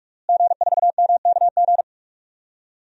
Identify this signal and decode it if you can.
Morse code, heard as a keyed tone, about 46 wpm, 700 Hz. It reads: G4MKG
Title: DXCC-100 Call Signs - 5 Characters - Repeated 3x - 45wpm